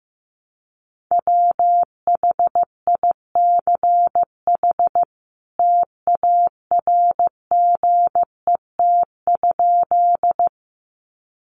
Morse code, heard as a keyed tone, about 15 wpm, 700 Hz. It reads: WHICH TARGET?
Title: Sentences from Top 1200 Words 15wpm